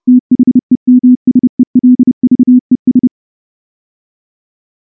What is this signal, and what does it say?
Morse code, heard as a keyed tone, about 28 wpm, 250 Hz. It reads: THEMSELVES
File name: morse-training-30WPM